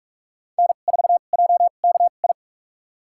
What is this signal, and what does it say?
Morse code, heard as a keyed tone, about 46 wpm, spaced as for 32 wpm, 700 Hz. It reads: N4JKI
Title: USA Call Signs - 5 Characters - 2x Character Spacing - 45wpm